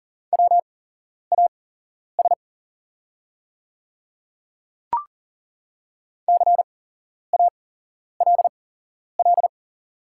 Morse code, heard as a keyed tone, about 40 wpm, spaced as for 11 wpm, 700 Hz. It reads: WAS CALL
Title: 8x Character Spacing - Top 200 Words - 40wpm